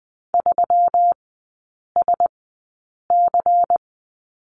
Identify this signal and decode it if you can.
Morse code, heard as a keyed tone, about 20 wpm, spaced as for 8 wpm, 690 Hz. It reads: 3SC
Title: 212 Lesson 34e - Number 8 - Three Characters 20wpm